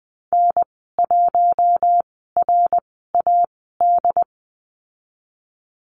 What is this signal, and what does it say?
Morse code, heard as a keyed tone, about 19 wpm, spaced as for 14 wpm, 700 Hz. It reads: N1RAD
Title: 122 Lesson 20g - Number 3 - Call Signs 20wpm